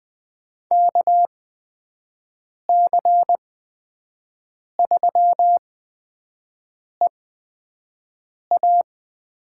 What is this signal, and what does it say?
Morse code, heard as a keyed tone, about 20 wpm, spaced as for 5 wpm, 700 Hz. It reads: KC3EA
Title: USA Call Signs - 8x Character Spacing - 20wpm